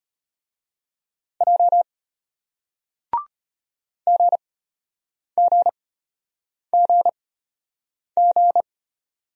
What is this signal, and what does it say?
Morse code, heard as a keyed tone, about 33 wpm, spaced as for 8 wpm, 700 Hz. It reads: J GGGG